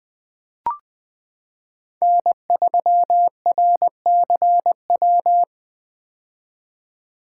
Morse code, heard as a keyed tone, about 20 wpm, 700 Hz. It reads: N3RCW
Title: 2023 POTA Hunter Call Signs - 20wpm